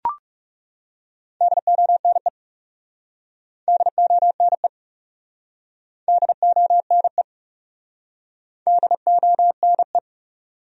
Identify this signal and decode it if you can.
Morse code, heard as a keyed tone, about 40 wpm, 700 Hz. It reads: DONE DONE DONE DONE